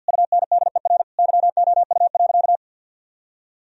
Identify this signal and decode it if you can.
Morse code, heard as a keyed tone, about 52 wpm, 700 Hz. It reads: UNDER YOU.
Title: Sentences from Top 1000 Words - Review Part 1 - 50wpm